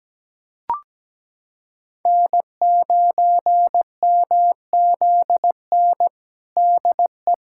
Morse code, heard as a keyed tone, about 17 wpm, 700 Hz. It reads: N9MZN DE
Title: USA Callsign DE Callsign - 5 Characters - 17wpm